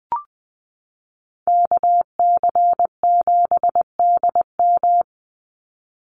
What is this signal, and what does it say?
Morse code, heard as a keyed tone, about 19 wpm, 700 Hz. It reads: KC7DM
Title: NA SOTA Activator Callsigns 20wpm